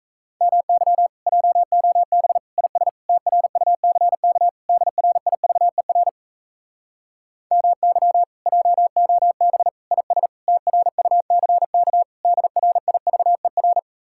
Morse code, heard as a keyed tone, about 37 wpm, 700 Hz. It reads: MY JOB IS TRUCK DRIVER MY JOB IS TRUCK DRIVER